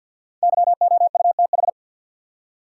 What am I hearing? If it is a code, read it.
Morse code, heard as a keyed tone, about 50 wpm, 700 Hz. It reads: YOUTH
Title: Five Letter Words - 50wpm